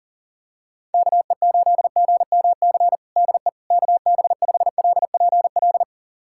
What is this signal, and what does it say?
Morse code, heard as a keyed tone, about 40 wpm, 700 Hz. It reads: KE8GMC DE KB5LPL
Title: USA Callsign DE Callsign - 6 Characters - 40wpm